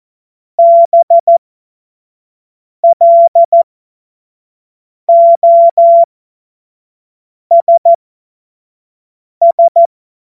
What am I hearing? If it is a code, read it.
Morse code, heard as a keyed tone, about 14 wpm, spaced as for 5 wpm, 680 Hz. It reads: BLOSS
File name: Morsekurs / Witze / Witz 03. Januar 2023 25 F70